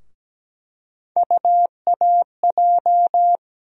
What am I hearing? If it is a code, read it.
Morse code, heard as a keyed tone, about 17 wpm, 700 Hz. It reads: UAJ
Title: USA Call Sign Postfixes - 3 Letters 17wpm